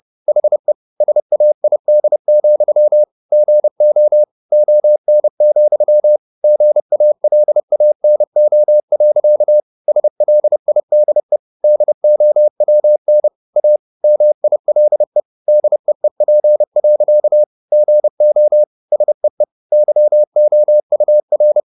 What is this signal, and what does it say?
Morse code, heard as a keyed tone, about 30 wpm, 600 Hz. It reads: HE SAID, GO ON, GALANO. SLIDE DOWN A MILE DEEP. GO SEE YOUR